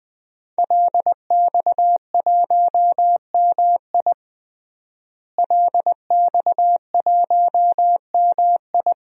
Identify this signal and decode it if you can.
Morse code, heard as a keyed tone, about 21 wpm, 700 Hz. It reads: LX1MI LX1MI
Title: DXCC-100 Call Signs - 5 Characters - Repeated 2x - 20wpm